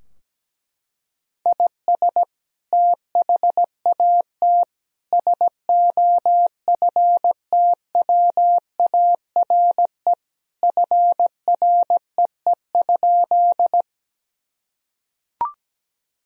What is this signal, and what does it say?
Morse code, heard as a keyed tone, about 17 wpm, 700 Hz. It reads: IS THAT SOFTWARE FREE?
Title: Sentences from Top 1000 Words - Review Part 3 - 17wpm